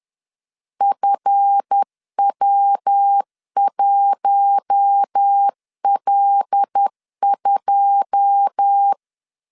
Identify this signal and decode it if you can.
Morse code, heard as a keyed tone, about 10 wpm, 800 Hz. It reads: FW1L2
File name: LEZIONI / 10 SIGNS / 10D COMA 10wpm 3013